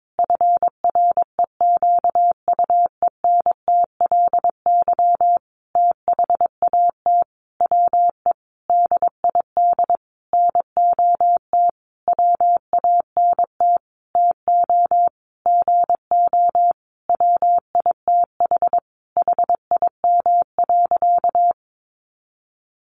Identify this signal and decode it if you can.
Morse code, heard as a keyed tone, about 22 wpm, 700 Hz. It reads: FREQUENTLY THAT WE DID NOT WANT TO GO WITH HIM.
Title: Sentences from Top 1300 Words 22wpm